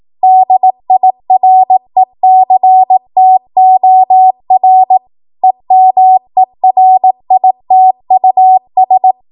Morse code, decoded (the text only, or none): DIRECTOR EMERITUS